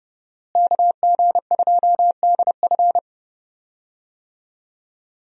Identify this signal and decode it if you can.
Morse code, heard as a keyed tone, about 30 wpm, 700 Hz. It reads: KG2DF
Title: USA Call Signs - 5 Characters - Repeated 3x - 30wpm